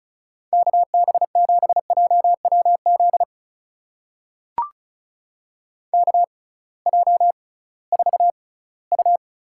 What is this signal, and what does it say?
Morse code, heard as a keyed tone, about 36 wpm, spaced as for 20 wpm, 700 Hz. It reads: KB7JWZ KJ4U
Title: USA Call Signs - 6 Characters - 6x Character Spacing - 35wpm